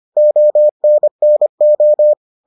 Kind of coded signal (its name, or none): Morse code